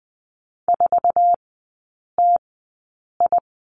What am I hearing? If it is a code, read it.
Morse code, heard as a keyed tone, about 20 wpm, spaced as for 8 wpm, 700 Hz. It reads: 4TI
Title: 060 Lesson 11e - Letter h - Three Characters 20wpm